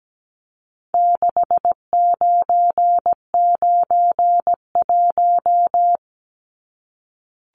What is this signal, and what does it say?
Morse code, heard as a keyed tone, about 17 wpm, 700 Hz. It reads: 6991
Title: Numbers - 4-Digit 17wpm